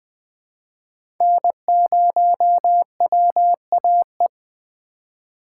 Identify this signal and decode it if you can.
Morse code, heard as a keyed tone, about 20 wpm, 700 Hz. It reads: N0WAE
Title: 2023 SOTA Chaser Call Signs - 20wpm